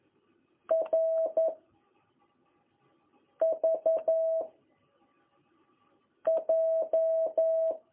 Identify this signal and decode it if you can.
Morse code, heard as a keyed tone, about 11 wpm, 660 Hz. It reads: R V J
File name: jh8cbh / 2021cw